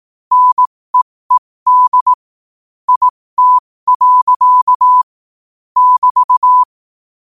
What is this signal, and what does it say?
Morse code, heard as a keyed tone, about 18 wpm, spaced as for 15 wpm, 1000 Hz. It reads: NEED IT. <BT>
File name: qotdfiles / 15wpm